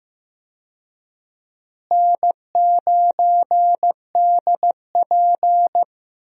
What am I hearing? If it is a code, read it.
Morse code, heard as a keyed tone, about 15 wpm, 700 Hz. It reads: N9DP
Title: USA Call Signs - 4 Characters - Repeated 3x - 15wpm